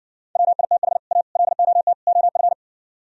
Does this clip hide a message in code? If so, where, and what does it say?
Yes: Morse code from 0.4 to 2.5 s, heard as a keyed tone, about 69 wpm, 700 Hz. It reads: WITH A LOT OF